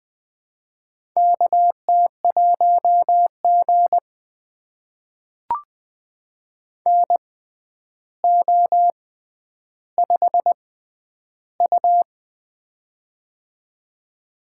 Morse code, heard as a keyed tone, about 20 wpm, spaced as for 10 wpm, 700 Hz. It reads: KT1G NO5U
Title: USA Call Signs - 4 Characters - 6x Character Spacing - 20wpm